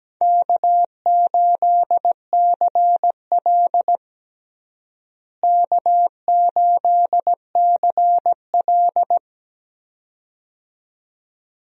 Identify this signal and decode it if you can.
Morse code, heard as a keyed tone, about 17 wpm, 700 Hz. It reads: K8CL K8CL
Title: USA Call Signs - 4 Characters - Repeated 2x - 17wpm